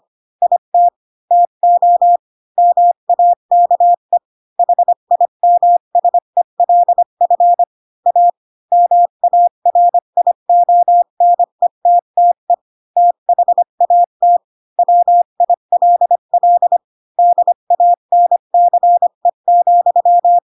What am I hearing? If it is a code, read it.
Morse code, heard as a keyed tone, about 25 wpm, 700 Hz. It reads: IT TO MAKE HIMSELF A MARIONETTE THAT WILL DANCE,
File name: Pinocchio / Pinocchio ch2 20 25 ik1hge